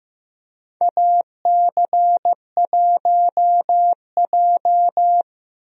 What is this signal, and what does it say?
Morse code, heard as a keyed tone, about 15 wpm, 700 Hz. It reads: AC1J
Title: USA Call Signs - 4 Characters - Repeated 3x - 15wpm